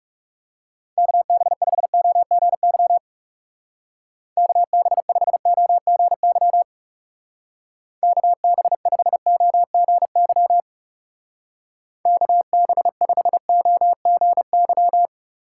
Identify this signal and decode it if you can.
Morse code, heard as a keyed tone, about 34 wpm, 700 Hz. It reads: KB5OGY KB5OGY KB5OGY KB5OGY
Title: Speed Racing - USA Call Signs 30wpm